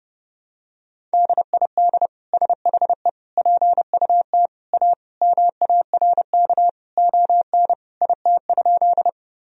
Morse code, heard as a keyed tone, about 32 wpm, 700 Hz. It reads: DID SHE PUT A MARK ON IT?
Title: Sentences from Top 1000 Words - Review Part 1 - 30wpm